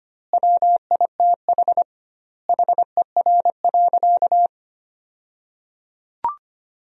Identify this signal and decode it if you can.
Morse code, heard as a keyed tone, about 25 wpm, 700 Hz. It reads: WITH HER.
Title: 2x Word-Spacing - Sentences from Top 100 Words - 25wpm